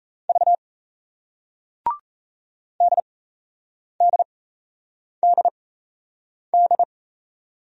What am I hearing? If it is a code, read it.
Morse code, heard as a keyed tone, about 38 wpm, spaced as for 8 wpm, 700 Hz. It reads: V DDDD